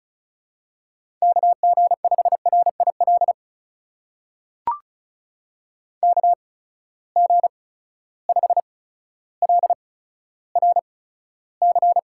Morse code, heard as a keyed tone, about 36 wpm, spaced as for 15 wpm, 700 Hz. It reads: KG5RIL KG5LRC
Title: USA Call Signs - 6 Characters - 8x Character Spacing - 35wpm